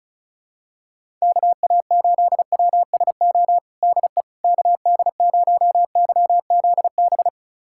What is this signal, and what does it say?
Morse code, heard as a keyed tone, about 36 wpm, 700 Hz. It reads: KA8WSO DE KD0YZB
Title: USA Callsign DE Callsign - 6 Characters - 35wpm